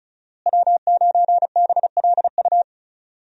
Morse code, heard as a keyed tone, about 34 wpm, 700 Hz. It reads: W9BLU
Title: USA Call Signs - 5 Characters - Repeated 2x - 35wpm